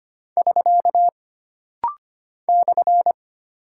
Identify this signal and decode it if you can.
Morse code, heard as a keyed tone, about 25 wpm, 700 Hz. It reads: <SK> /
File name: Single Punctuation-Prosign - Rapid-Fire - 25wpm